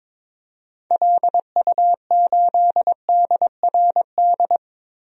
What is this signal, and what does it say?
Morse code, heard as a keyed tone, about 22 wpm, 700 Hz. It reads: LU8DRD
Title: DXCC-100 Call Signs - 6 Characters - Repeated 3x - 22wpm